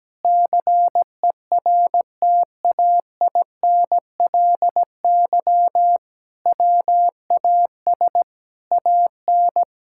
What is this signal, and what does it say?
Morse code, heard as a keyed tone, about 17 wpm, 700 Hz. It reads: CERTAINLY WAS AN
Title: Sentences from Top 1000 Words - Review Part 6 - 17wpm